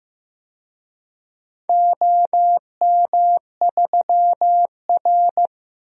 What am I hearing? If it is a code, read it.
Morse code, heard as a keyed tone, about 15 wpm, 700 Hz. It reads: OM3R
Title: DXCC-100 Call Signs - 4 Characters - Repeated 2x - 15wpm